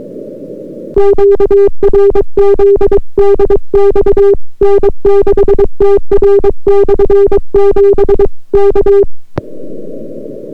Morse code, heard as a keyed tone, about 23 wpm, 400 Hz. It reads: QRZ DX N6TR/7 K